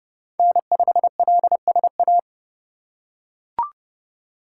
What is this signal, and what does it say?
Morse code, heard as a keyed tone, about 32 wpm, 700 Hz. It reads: N5LSA
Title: USA Call Signs - 5 Characters - 4x Character Spacing - 30wpm